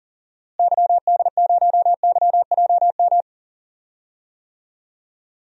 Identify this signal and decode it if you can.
Morse code, heard as a keyed tone, about 37 wpm, 700 Hz. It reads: YD0YJM
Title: DXCC-100 Call Signs - 6 Characters - Repeated 3x - 40wpm